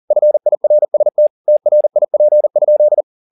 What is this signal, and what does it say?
Morse code, heard as a keyed tone, about 40 wpm, 600 Hz. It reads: FIRST TRIP?